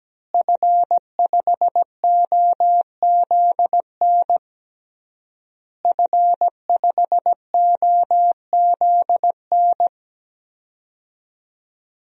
Morse code, heard as a keyed tone, about 17 wpm, 700 Hz. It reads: F5OZN F5OZN
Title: DXCC-100 Call Signs - 5 Characters - Repeated 3x - 17wpm